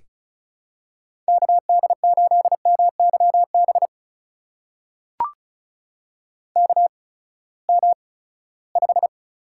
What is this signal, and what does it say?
Morse code, heard as a keyed tone, about 36 wpm, spaced as for 20 wpm, 700 Hz. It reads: KD8MYB KM5
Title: USA Call Signs - 8x Character Spacing - 35wpm